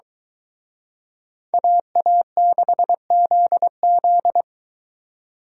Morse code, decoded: AA6ZZ

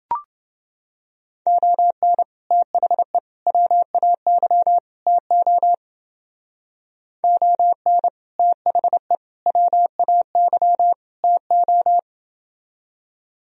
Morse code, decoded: ON THE WAY TO ON THE WAY TO

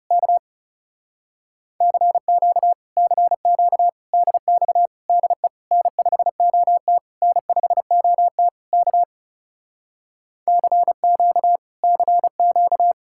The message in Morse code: K CQ CQ DX DE N5OT N5OT K CQ CQ

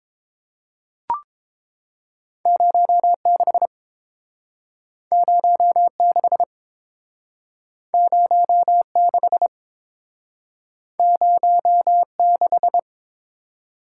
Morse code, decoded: 06 06 06 06

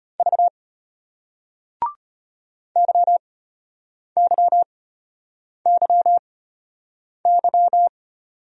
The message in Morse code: V YYYY

